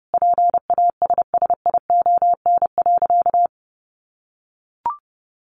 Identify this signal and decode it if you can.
Morse code, heard as a keyed tone, about 30 wpm, 700 Hz. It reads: PASSION.
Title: Sentences from Top 1300 Words 30wpm